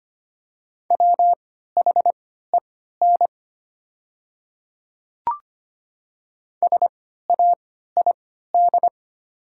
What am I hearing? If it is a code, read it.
Morse code, heard as a keyed tone, about 25 wpm, spaced as for 14 wpm, 700 Hz. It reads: WHEN SAID
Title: 3x Character Spacing - Top 200 Words - 25wpm